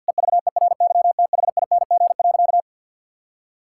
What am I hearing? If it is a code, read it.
Morse code, heard as a keyed tone, about 52 wpm, 700 Hz. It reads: EVERYTHING.
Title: Sentences from Top 1000 Words - Review Part 5 - 50wpm